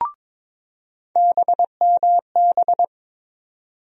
Morse code, heard as a keyed tone, about 22 wpm, 700 Hz. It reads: BMB
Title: USA Call Sign Postfixes - 3 Letters 22wpm